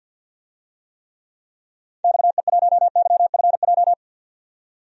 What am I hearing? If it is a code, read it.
Morse code, heard as a keyed tone, about 46 wpm, 700 Hz. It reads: XE1YFJ
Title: DXCC-100 Call Signs - 6 Characters - 50wpm